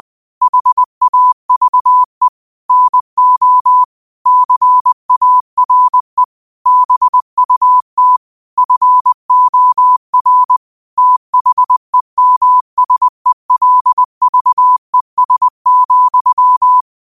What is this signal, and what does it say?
Morse code, heard as a keyed tone, about 20 wpm, 1000 Hz. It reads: HAVE NO CARE BUT FOR THEMSELVES,
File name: qotdfiles / 20wpm